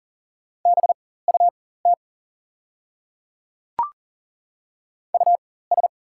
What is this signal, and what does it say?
Morse code, heard as a keyed tone, about 40 wpm, spaced as for 19 wpm, 690 Hz. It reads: BUT US
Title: 4x Character Spacing - Top 200 Words - 40wpm